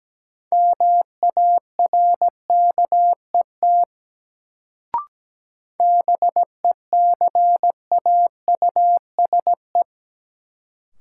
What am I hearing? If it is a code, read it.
Morse code, heard as a keyed tone, about 17 wpm, 700 Hz. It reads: MARKET BECAUSE
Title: Sentences from Top 600 Words - New Word Drill - 17wpm